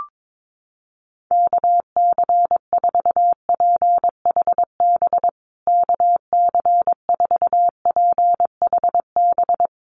Morse code, heard as a keyed tone, about 22 wpm, 700 Hz. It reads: KC4PHB KC4PHB
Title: USA Call Signs Repeated 2x 22wpm